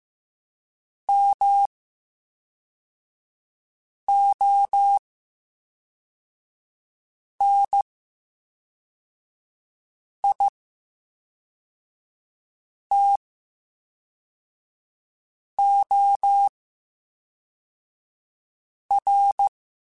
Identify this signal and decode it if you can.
Morse code, heard as a keyed tone, about 15 wpm, spaced as for 3 wpm, 800 Hz. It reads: MONITOR
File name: Special INGs / Special INGs-mp3s / VerySlowToFaster Ing @10